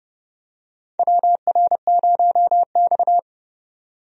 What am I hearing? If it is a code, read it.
Morse code, heard as a keyed tone, about 30 wpm, 700 Hz. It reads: WR0X
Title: USA Call Signs - 4 Characters - 30wpm